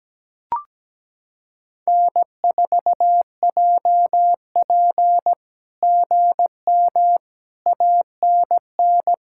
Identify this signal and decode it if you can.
Morse code, heard as a keyed tone, about 17 wpm, 700 Hz. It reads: N4JP GM ANN
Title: SOTA QSO Practice 17wpm